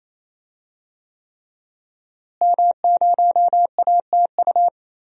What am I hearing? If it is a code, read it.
Morse code, heard as a keyed tone, about 28 wpm, 700 Hz. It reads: M0ATU
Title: DXCC-100 Call Signs - 5 Characters - Repeated 3x - 28wpm